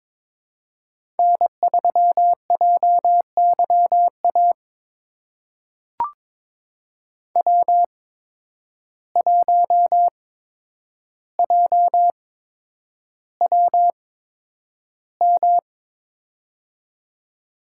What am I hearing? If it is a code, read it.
Morse code, heard as a keyed tone, about 22 wpm, spaced as for 9 wpm, 700 Hz. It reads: N3JYA W1JWM
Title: USA Call Signs - 5 Characters - 8x Character Spacing - 22wpm